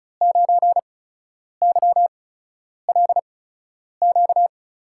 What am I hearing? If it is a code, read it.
Morse code, heard as a keyed tone, about 36 wpm, spaced as for 10 wpm, 700 Hz. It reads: 9YLQ
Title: USA Call Signs - 5 Characters - 8x Character Spacing - 35wpm